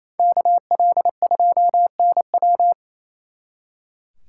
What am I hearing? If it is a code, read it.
Morse code, heard as a keyed tone, about 34 wpm, 700 Hz. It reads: KL2NW
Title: Speed Racing - USA Call Signs 28wpm